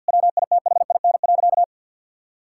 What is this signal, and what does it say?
Morse code, heard as a keyed tone, about 50 wpm, 700 Hz. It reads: WITHIN.
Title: Sentences from Top 1600 Words 50wpm